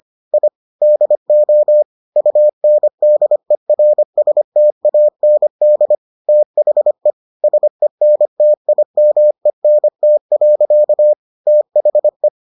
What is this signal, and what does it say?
Morse code, heard as a keyed tone, about 25 wpm, 600 Hz. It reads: I DO UNDERSTAND THE SENTIMENT. THE